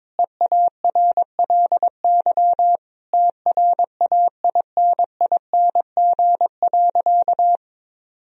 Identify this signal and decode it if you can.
Morse code, heard as a keyed tone, about 21 wpm, 700 Hz. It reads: EARLY TRAINING.
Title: Sentences from Top 1000 Words - Review Part 4 - 22wpm